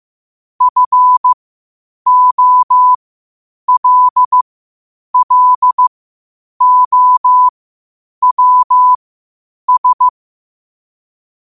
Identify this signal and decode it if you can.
Morse code, heard as a keyed tone, about 15 wpm, spaced as for 8 wpm, 1000 Hz. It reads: FOLLOWS
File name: qotdfiles / 8wpm